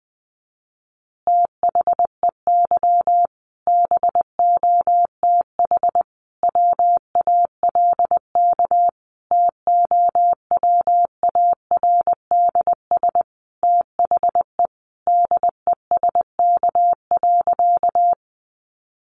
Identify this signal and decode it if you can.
Morse code, heard as a keyed tone, about 20 wpm, 700 Hz. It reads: THEY BOTH WALK TOWARDS THE DESK.
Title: Additional Easy Phrases 20wpm